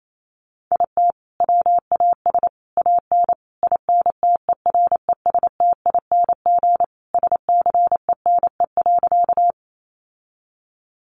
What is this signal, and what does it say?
Morse code, heard as a keyed tone, about 28 wpm, 700 Hz. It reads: IT WAS AN INTERESTING SCENE.
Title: Sentences from Top 1100 Words 28wpm